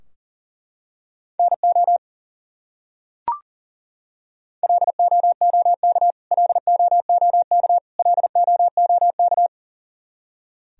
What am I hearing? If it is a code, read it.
Morse code, heard as a keyed tone, about 40 wpm, 700 Hz. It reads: NO LOOK LOOK LOOK
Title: Top 100 Words Repeated 40wpm